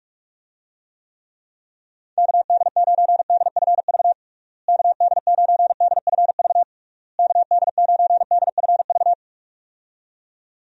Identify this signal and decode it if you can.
Morse code, heard as a keyed tone, about 46 wpm, 700 Hz. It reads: KD9DFV KD9DFV KD9DFV
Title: USA Call Signs - 6 Characters - Repeated 3x - 45wpm